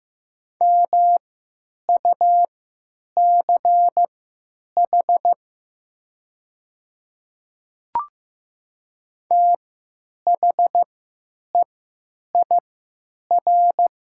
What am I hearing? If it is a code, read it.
Morse code, heard as a keyed tone, about 15 wpm, spaced as for 9 wpm, 700 Hz. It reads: MUCH THEIR
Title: 3x Character Spacing - Top 200 Words - 15wpm